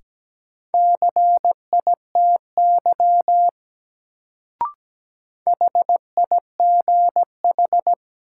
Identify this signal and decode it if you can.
Morse code, heard as a keyed tone, about 17 wpm, 700 Hz. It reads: CITY HIGH
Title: Sets of 2 Words - New Word Drill - 17wpm